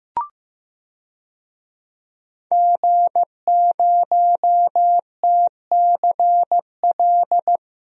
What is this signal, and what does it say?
Morse code, heard as a keyed tone, about 15 wpm, 700 Hz. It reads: G0TCL
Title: DXCC-100 Call Signs - 5 Characters - Repeated 2x - 15wpm